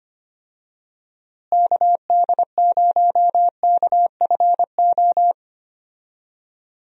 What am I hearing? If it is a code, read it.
Morse code, heard as a keyed tone, about 25 wpm, 700 Hz. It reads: KD0KFO
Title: USA Call Signs - 6 Characters - Repeated 3x - 25wpm